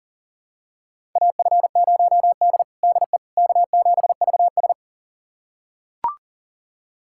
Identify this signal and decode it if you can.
Morse code, heard as a keyed tone, about 40 wpm, 700 Hz. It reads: AF0D DE K7VS
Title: USA Callsign DE Callsign - 4 Characters - 40wpm